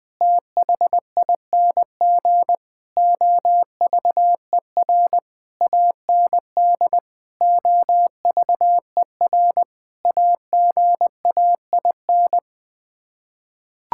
Morse code, heard as a keyed tone, about 20 wpm, 700 Hz. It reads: THING OVER AND OVER AGAIN
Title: Sets of 8 Words - 20wpm